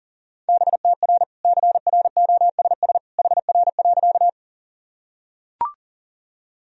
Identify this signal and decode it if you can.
Morse code, heard as a keyed tone, about 40 wpm, 700 Hz. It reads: BTR CROSS HR.
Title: Sentences from Top 400 Words and Abbreviations 40wpm